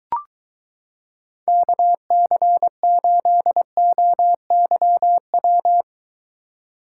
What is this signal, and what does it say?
Morse code, heard as a keyed tone, about 23 wpm, 700 Hz. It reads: KC8OYW